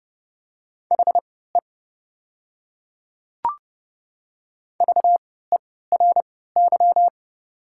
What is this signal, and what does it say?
Morse code, heard as a keyed tone, about 31 wpm, spaced as for 17 wpm, 700 Hz. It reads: HE VERY
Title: 3x Character Spacing - Top 200 Words - 30wpm